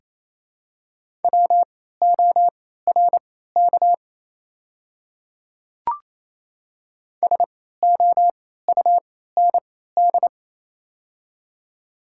Morse code, heard as a keyed tone, about 28 wpm, spaced as for 16 wpm, 700 Hz. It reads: WORK SOUND
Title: 3x Character Spacing - Top 200 Words - 28wpm